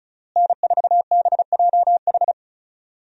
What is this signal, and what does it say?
Morse code, heard as a keyed tone, about 36 wpm, 700 Hz. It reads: N4BJH
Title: USA Call Signs - 5 Characters - Repeated 3x - 35wpm